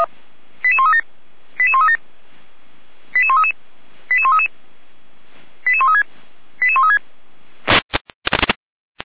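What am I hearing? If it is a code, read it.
Five-tone selective call (ZVEI), heard as quick runs of five tones, tones of about 70 ms. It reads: 88137 88137 88133 88133 88136 88136
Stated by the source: ZVEI five-tone selective call received by SDR, from the audio server of feuerwehr-orlamuende.org (fire brigade Orlamünde).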